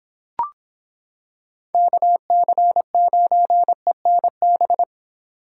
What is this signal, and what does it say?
Morse code, heard as a keyed tone, about 26 wpm, 700 Hz. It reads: KC9ENB